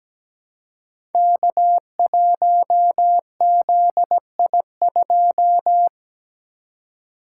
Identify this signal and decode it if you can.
Morse code, heard as a keyed tone, about 17 wpm, 700 Hz. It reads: K1ZI2
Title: Code Groups - 5 Letter:Numbers - 17wpm